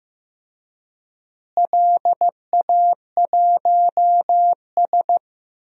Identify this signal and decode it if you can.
Morse code, heard as a keyed tone, about 15 wpm, 700 Hz. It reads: LA1S
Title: DXCC-100 Call Signs - 4 Characters - Repeated 2x - 15wpm